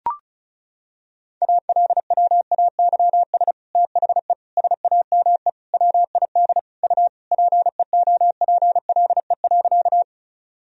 Morse code, heard as a keyed tone, about 36 wpm, 700 Hz. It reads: ALWAYS THE SAME WID U PEOPLE.
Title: Sentences from Top 300 Words and Abbreviations 35wpm